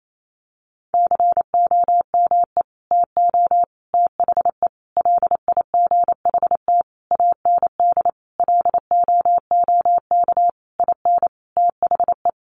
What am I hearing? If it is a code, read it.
Morse code, heard as a keyed tone, about 28 wpm, 700 Hz. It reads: COME TO THE LIGHT AND LOOK IN THE
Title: Sentences from Top 1300 Words 28wpm